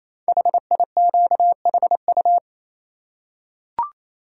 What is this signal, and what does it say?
Morse code, heard as a keyed tone, about 28 wpm, 700 Hz. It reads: HIQHU
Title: Code Groups - 5 Letters - 28wpm